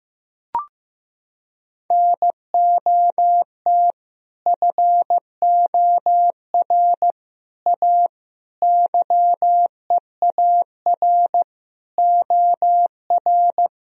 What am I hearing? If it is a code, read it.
Morse code, heard as a keyed tone, about 15 wpm, 700 Hz. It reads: NOT FOR A YEAR OR
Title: Sentences from Top 1000 Words - Review Part 1 - 15wpm